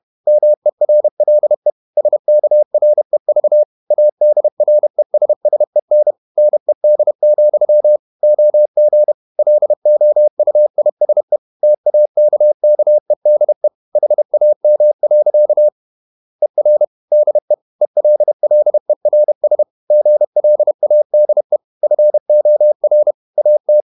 Morse code, heard as a keyed tone, about 31 wpm, 600 Hz. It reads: MERLE SKREV ADRESSEN NED, OG LOUISE TAKKEDE HAM. ER DE ELLERS GLADE FOR AT
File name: Ferie QRQ 31wpm